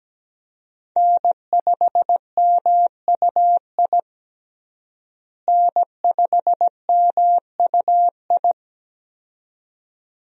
Morse code, heard as a keyed tone, about 17 wpm, 700 Hz. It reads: N5MUI N5MUI
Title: USA Call Signs - 5 Characters - Repeated 2x - 17wpm